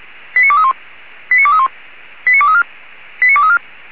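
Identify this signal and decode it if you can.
Five-tone selective call (ZVEI), heard as quick runs of five tones, tones of about 70 ms. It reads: 89231 89231 89234 89234
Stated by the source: ZVEI five-tone selective call received by SDR.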